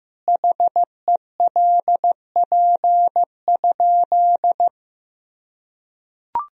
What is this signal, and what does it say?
Morse code, heard as a keyed tone, about 15 wpm, 700 Hz. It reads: HELP?
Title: Sentences from Top 1000 Words - Review Part 4 - 15wpm